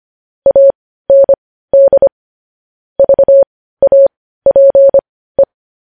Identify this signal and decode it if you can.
Morse code, heard as a keyed tone, about 25 wpm, spaced as for 15 wpm, 550 Hz. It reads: AND VAPE
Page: news-headlines-in-morse-code-at-15-wpm